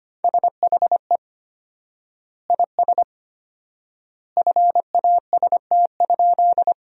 Morse code, heard as a keyed tone, about 25 wpm, 700 Hz. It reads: SHE IS FAST?
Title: 4x Word-Spacing - Sentences from Top 400 Words - 25wpm